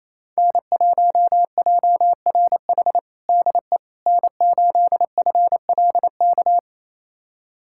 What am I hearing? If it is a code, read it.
Morse code, heard as a keyed tone, about 28 wpm, 700 Hz. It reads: N1JRH DE N8FLK
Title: USA Callsign DE Callsign - 5 Characters - 28wpm